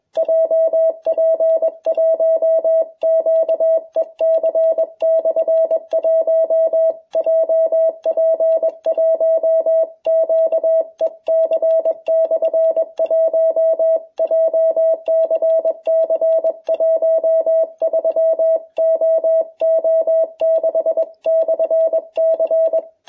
Morse code, heard as a keyed tone, about 22 wpm, 660 Hz. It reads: JP1QEC/1 JP1QEC/1 JCC 13OO6/C